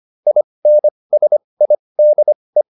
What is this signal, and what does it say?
Morse code, heard as a keyed tone, about 25 wpm, spaced as for 20 wpm, 600 Hz. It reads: INSIDE